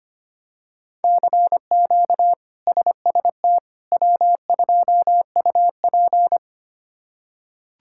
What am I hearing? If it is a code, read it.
Morse code, heard as a keyed tone, about 25 wpm, 700 Hz. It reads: CQ SST W2UP
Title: K1USN SST QSO Practice - 25wpm